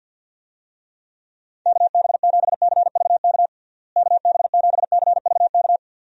Morse code, heard as a keyed tone, about 52 wpm, 700 Hz. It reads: KB7CVK KB7CVK
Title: USA Call Signs - 6 Characters - Repeated 2x - 50wpm